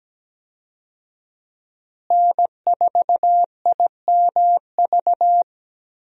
Morse code, heard as a keyed tone, about 17 wpm, 700 Hz. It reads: N4IMV
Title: USA Call Signs - 5 Characters - Repeated 2x - 17wpm